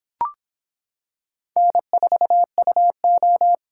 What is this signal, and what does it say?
Morse code, heard as a keyed tone, about 26 wpm, 700 Hz. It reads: N4UO